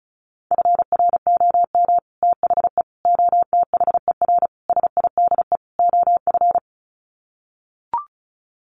Morse code, heard as a keyed tone, about 34 wpm, 700 Hz. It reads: FROM THE OTHER SIDE OF
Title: Sets of 5 Words Encore 35wpm